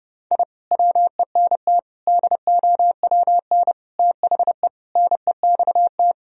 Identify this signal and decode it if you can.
Morse code, heard as a keyed tone, about 32 wpm, 700 Hz. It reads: I WENT DOWN THE NEXT
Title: Sentences from Top 1000 Words - Review Part 1 - 30wpm